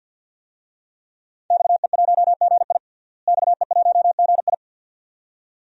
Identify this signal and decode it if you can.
Morse code, heard as a keyed tone, about 47 wpm, 700 Hz. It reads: XE1GI XE1GI
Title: DXCC-100 Call Signs - 5 Characters - Repeated 2x - 50wpm